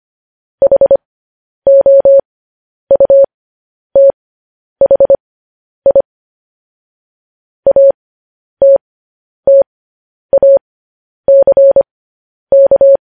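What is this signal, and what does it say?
Morse code, heard as a keyed tone, about 25 wpm, spaced as for 10 wpm, 550 Hz. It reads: HOUTHI ATTACK
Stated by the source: Morse code sending of news headlines, speed 10.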